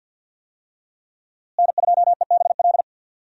Morse code, heard as a keyed tone, about 52 wpm, 700 Hz. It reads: N2EBL
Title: USA Call Signs - 5 Characters - Repeated 2x - 50wpm